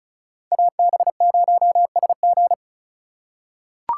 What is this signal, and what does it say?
Morse code, heard as a keyed tone, about 36 wpm, 700 Hz. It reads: AB0SG
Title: USA Call Signs - 5 Characters - 3x Character Spacing - 35wpm